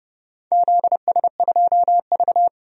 Morse code, heard as a keyed tone, about 32 wpm, 700 Hz. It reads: ZS2V
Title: DXCC-100 Call Signs - 4 Characters - 30wpm